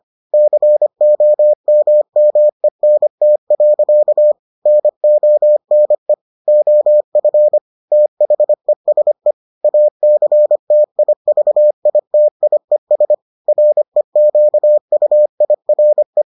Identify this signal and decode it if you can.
Morse code, heard as a keyed tone, about 25 wpm, 600 Hz. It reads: COMMENT. NONE OF THESE ACTIVITIES REQUIRE